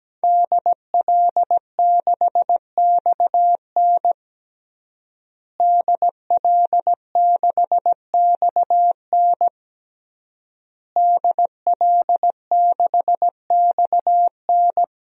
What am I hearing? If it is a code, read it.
Morse code, heard as a keyed tone, about 17 wpm, 700 Hz. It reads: DL6XN DL6XN DL6XN
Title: DXCC-100 Call Signs - 5 Characters - Repeated 3x - 17wpm